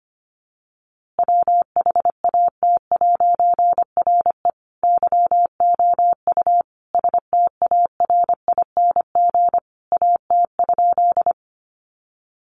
Morse code, heard as a keyed tone, about 25 wpm, 700 Hz. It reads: WHAT'RE YOU STARING AT?
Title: A Sound Of Thunder 25wpm